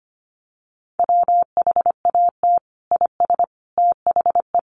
Morse code, heard as a keyed tone, about 25 wpm, 700 Hz. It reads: WHAT IS THE
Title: Sentences from Top 1300 Words 25wpm